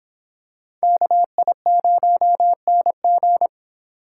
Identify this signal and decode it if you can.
Morse code, heard as a keyed tone, about 25 wpm, 700 Hz. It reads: KI0NG